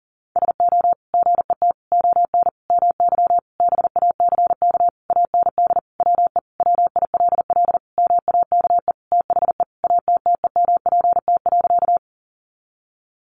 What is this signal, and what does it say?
Morse code, heard as a keyed tone, about 40 wpm, 700 Hz. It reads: SO GET ON MY BACK AND WE WILL MAKE THE ATTEMPT.
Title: Sentences from Top 1000 Words 40wpm